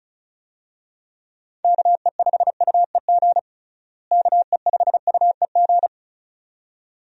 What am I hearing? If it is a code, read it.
Morse code, heard as a keyed tone, about 36 wpm, 700 Hz. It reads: KE5UEG KE5UEG
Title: USA Call Signs - 6 Characters - Repeated 2x - 35wpm